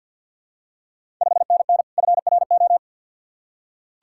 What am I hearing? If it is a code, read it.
Morse code, heard as a keyed tone, about 48 wpm, 690 Hz. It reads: 5NN FRO
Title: Ontario QSO Party Exchange 50wpm